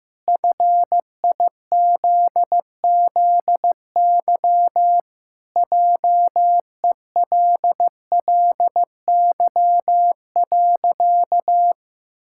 Morse code, heard as a keyed tone, about 16 wpm, 700 Hz. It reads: FIZZY JELLY.
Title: Pangram Challenge v2 - 15wpm